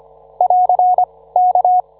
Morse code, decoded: <AR> K